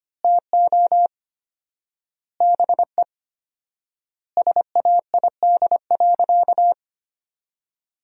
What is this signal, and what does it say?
Morse code, heard as a keyed tone, about 25 wpm, 700 Hz. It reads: TO BE SAID.
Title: 4x Word-Spacing - Sentences from Top 500 Words - 25wpm